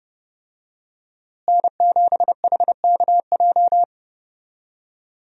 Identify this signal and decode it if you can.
Morse code, heard as a keyed tone, about 30 wpm, 700 Hz. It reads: N7HKJ
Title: USA Call Signs - 5 Characters - Repeated 3x - 30wpm